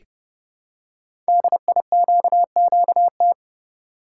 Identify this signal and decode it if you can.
Morse code, heard as a keyed tone, about 30 wpm, 700 Hz. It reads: DIQQT
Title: Code Groups - 5 Letters - 30wpm